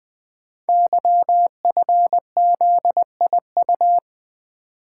Morse code, heard as a keyed tone, about 20 wpm, 700 Hz. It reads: YFZIU
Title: Code Groups - 5 Letters - 20wpm